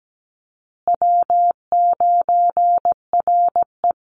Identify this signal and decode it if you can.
Morse code, heard as a keyed tone, about 17 wpm, 700 Hz. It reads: W9RE
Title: Call Signs Repeated 2x - 17wpm